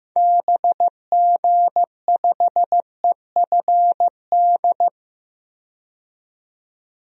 Morse code, heard as a keyed tone, about 15 wpm, 700 Hz. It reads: BG5EFD
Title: DXCC-100 Call Signs - 6 Characters - 15wpm